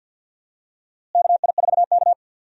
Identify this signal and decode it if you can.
Morse code, heard as a keyed tone, about 52 wpm, 700 Hz. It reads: KI4K
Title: USA Call Signs - 4 Characters - Repeated 3x - 50wpm